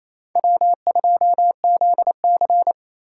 Morse code, heard as a keyed tone, about 27 wpm, 700 Hz. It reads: W2ZC
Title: USA Call Signs - 4 Characters - 3x Character Spacing - 28wpm